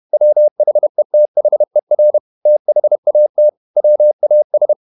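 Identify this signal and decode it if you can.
Morse code, heard as a keyed tone, about 31 wpm, 600 Hz. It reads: WHETHER THAT WAS
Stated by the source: Morse code at 30 wpm, the novel Never by Ken Follett in English, from CW med Gnister.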